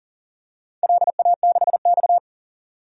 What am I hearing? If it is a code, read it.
Morse code, heard as a keyed tone, about 40 wpm, 700 Hz. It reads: LA6X
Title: DXCC-100 Call Signs - 4 Characters - 40wpm